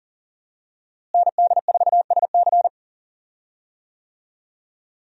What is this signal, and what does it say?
Morse code, heard as a keyed tone, about 40 wpm, 700 Hz. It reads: ND4SC
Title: USA Call Signs - 5 Characters - Repeated 2x - 40wpm